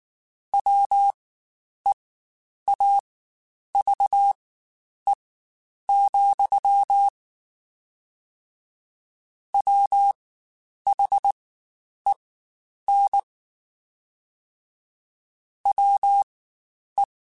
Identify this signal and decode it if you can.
Morse code, heard as a keyed tone, about 19 wpm, spaced as for 9 wpm, 800 Hz. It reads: WEAVE, WHEN WE
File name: Quotes Spaced / Quotes Spaced-mp3s / Deceive sp @18